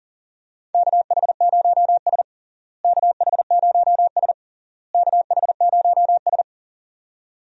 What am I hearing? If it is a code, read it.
Morse code, heard as a keyed tone, about 40 wpm, 700 Hz. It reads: KH0S KH0S KH0S
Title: USA Call Signs - 4 Characters - Repeated 3x - 40wpm